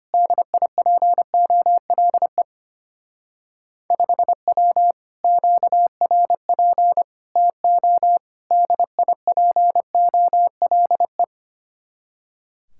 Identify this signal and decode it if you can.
Morse code, heard as a keyed tone, about 31 wpm, 700 Hz. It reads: DIPOLE 5W QRP TO DIPOLE
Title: Speed-Racing Common QSO Phrases - 25wpm